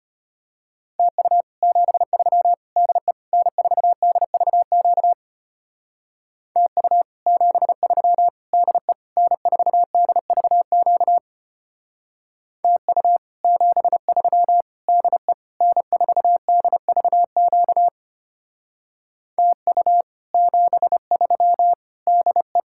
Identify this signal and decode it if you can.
Morse code, heard as a keyed tone, about 32 wpm, 700 Hz. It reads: TU 73 DE N4DVQ TU 73 DE N4DVQ TU 73 DE N4DVQ TU 73 DE